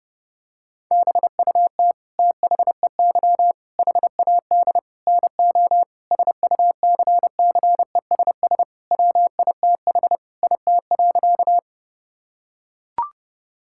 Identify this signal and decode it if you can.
Morse code, heard as a keyed tone, about 30 wpm, 700 Hz. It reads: BUT THEY HAD NO SUCCESS WITH IT.
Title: Sentences from Top 1000 Words - Review Part 6 - 30wpm